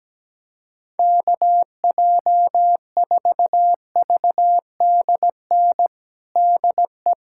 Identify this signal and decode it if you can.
Morse code, heard as a keyed tone, about 17 wpm, 700 Hz. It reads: KJ4VDN DE
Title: USA Callsign DE Callsign - 6 Characters - 17wpm